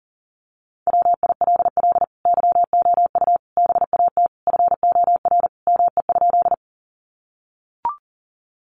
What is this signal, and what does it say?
Morse code, heard as a keyed tone, about 40 wpm, 700 Hz. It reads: WILL YOU BAT FOR ME?
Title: Sentences from Top 1500 Words 40wpm